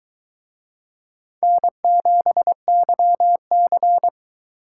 Morse code, heard as a keyed tone, about 23 wpm, 700 Hz. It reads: N7YC